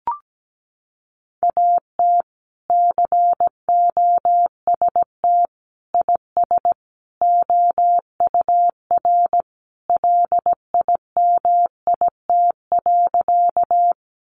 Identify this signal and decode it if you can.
Morse code, heard as a keyed tone, about 17 wpm, 700 Hz. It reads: AT COST IS OUR LIMIT.
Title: Sentences from Top 1500 Words 17wpm